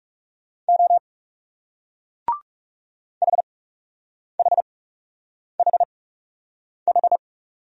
Morse code, heard as a keyed tone, about 40 wpm, spaced as for 8 wpm, 700 Hz. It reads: O HHHH